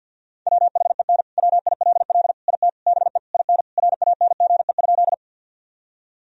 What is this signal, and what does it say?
Morse code, heard as a keyed tone, about 50 wpm, 700 Hz. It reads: WHEN WILL IT BE IN RANGE?
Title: Sentences from Top 1000 Words - Review Part 4 - 50wpm